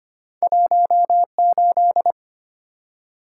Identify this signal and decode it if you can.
Morse code, heard as a keyed tone, about 25 wpm, 700 Hz. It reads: 18
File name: Two Letter-Number Combinations - 25wpm